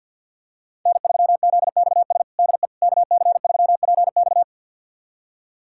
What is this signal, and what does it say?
Morse code, heard as a keyed tone, about 53 wpm, 700 Hz. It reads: N3ZXS DE KC3PX
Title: USA Callsign DE Callsign - 5 Characters - 50wpm